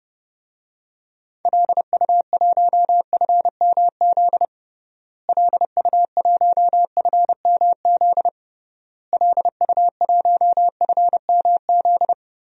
Morse code, heard as a keyed tone, about 30 wpm, 700 Hz. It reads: LU1FMZ LU1FMZ LU1FMZ
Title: DXCC-100 Call Signs - 6 Characters - Repeated 3x - 30wpm